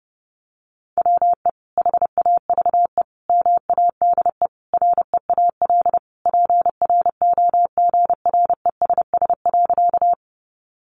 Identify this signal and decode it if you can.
Morse code, heard as a keyed tone, about 30 wpm, 700 Hz. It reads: WE HAVE MADE REAL PROGRESS.
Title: Sentences from Top 1400 Words 30wpm